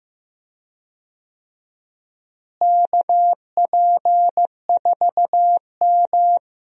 Morse code, heard as a keyed tone, about 15 wpm, 700 Hz. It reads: KP4M
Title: DXCC-100 Call Signs - 4 Characters - Repeated 2x - 15wpm